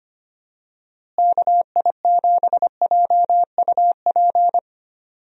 Morse code, heard as a keyed tone, about 25 wpm, 700 Hz. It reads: KI7JUP
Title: USA Call Signs - 6 Characters - Repeated 2x - 25wpm